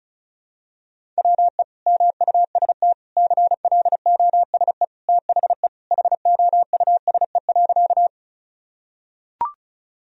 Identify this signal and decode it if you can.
Morse code, heard as a keyed tone, about 34 wpm, 700 Hz. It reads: WE MUST CLOSE THE HOUSE.
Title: Sentences from Top 1000 Words - Review Part 1 - 35wpm